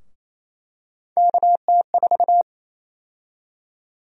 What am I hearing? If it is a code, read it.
Morse code, heard as a keyed tone, about 28 wpm, 700 Hz. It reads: KT4
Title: USA Call Sign Prefixes - 28wpm